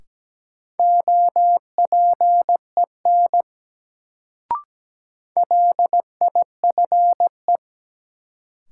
Morse code, heard as a keyed tone, about 17 wpm, 700 Hz. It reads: OPEN LIFE
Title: Sentences from Top 300 Words - New Word Drill - 17wpm